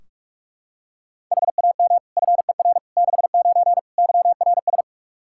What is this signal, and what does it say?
Morse code, heard as a keyed tone, about 41 wpm, 700 Hz. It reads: HAM FER 69 YRS
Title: K1USN SST QSO Practice 45wpm